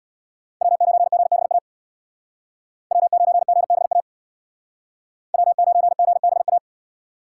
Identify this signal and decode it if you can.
Morse code, heard as a keyed tone, about 65 wpm, 700 Hz. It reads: W9GBA W9GBA W9GBA